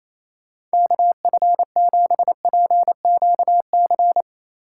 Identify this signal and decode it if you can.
Morse code, heard as a keyed tone, about 37 wpm, 700 Hz. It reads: KF7PQC